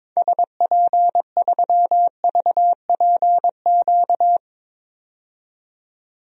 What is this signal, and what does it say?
Morse code, heard as a keyed tone, about 22 wpm, 700 Hz. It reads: SP3VPQ
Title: DXCC-100 Call Signs - 6 Characters - 22wpm